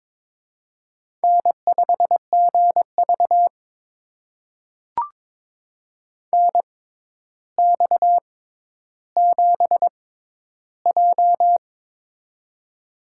Morse code, heard as a keyed tone, about 22 wpm, spaced as for 11 wpm, 700 Hz. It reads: N5GV NX7J
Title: USA Call Signs - 6x Character Spacing - 22wpm